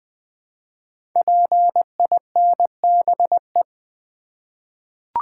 Morse code, heard as a keyed tone, about 20 wpm, 700 Hz. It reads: PINBE
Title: Code Groups - 5 Letter:Numbers - 20wpm